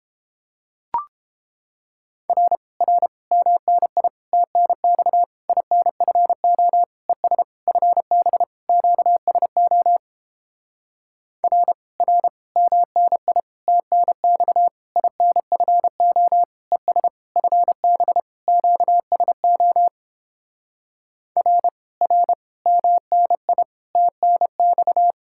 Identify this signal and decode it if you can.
Morse code, heard as a keyed tone, about 29 wpm, 700 Hz. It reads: R R MNI TNX INFO ES FB QSO R R MNI TNX INFO ES FB QSO R R MNI TNX